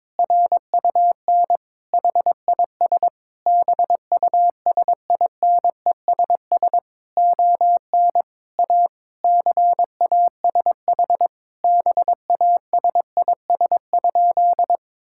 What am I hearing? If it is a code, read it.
Morse code, heard as a keyed tone, about 22 wpm, 700 Hz. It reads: RUN HIS BUSINESS ON A CASH BASIS?
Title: Sentences from Top 1000 Words - Review Part 5 - 22wpm